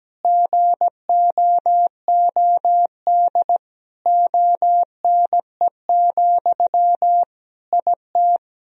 Morse code, heard as a keyed tone, about 17 wpm, 700 Hz. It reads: GOOD ONE, IT
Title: Sentences from Top 1000 Words - Review Part 1 - 17wpm